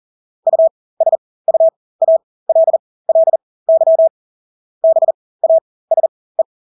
Morse code, heard as a keyed tone, about 40 wpm, spaced as for 20 wpm, 650 Hz. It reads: USUALLY BASE
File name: Engelske ord 40-20 wpm